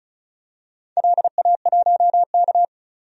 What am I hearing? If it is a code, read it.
Morse code, heard as a keyed tone, about 34 wpm, 700 Hz. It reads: LA1K
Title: DXCC-100 Call Signs - 4 Characters - 35wpm